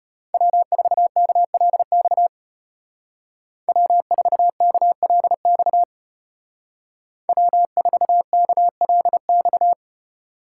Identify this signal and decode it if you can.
Morse code, heard as a keyed tone, about 33 wpm, 700 Hz. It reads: W4KLX W4KLX W4KLX